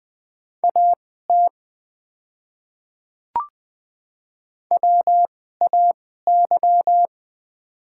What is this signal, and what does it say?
Morse code, heard as a keyed tone, about 20 wpm, spaced as for 15 wpm, 700 Hz. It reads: AT WAY
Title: 2x Character Spacing - Top 100 Words - 20wpm